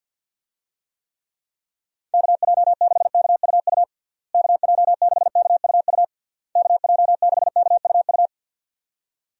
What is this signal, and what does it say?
Morse code, heard as a keyed tone, about 52 wpm, 700 Hz. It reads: KJ6KUU KJ6KUU KJ6KUU
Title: USA Call Signs - 6 Characters - Repeated 3x - 50wpm